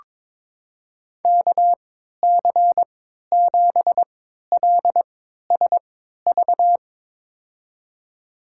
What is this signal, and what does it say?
Morse code, heard as a keyed tone, about 22 wpm, spaced as for 13 wpm, 700 Hz. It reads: KC7LSV
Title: USA Call Signs - 6 Characters - 3x Character Spacing - 22wpm